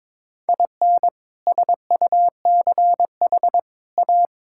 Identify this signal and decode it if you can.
Morse code, heard as a keyed tone, about 22 wpm, 700 Hz. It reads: IN SUCH A
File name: practice / 22wpm / Sets of 7 Words - 22wpm